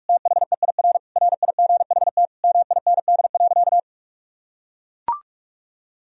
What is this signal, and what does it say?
Morse code, heard as a keyed tone, about 42 wpm, 700 Hz. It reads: THEIR RIGHT MIND.
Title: Sentences from Top 1000 Words - Review Part 1 - 45wpm